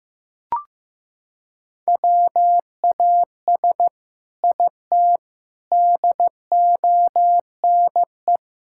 Morse code, heard as a keyed tone, about 14 wpm, 700 Hz. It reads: WAS IT DONE
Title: Sentences from Top 1000 Words - Review Part 6 - 15wpm